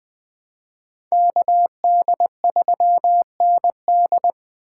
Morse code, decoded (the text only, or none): KD3ND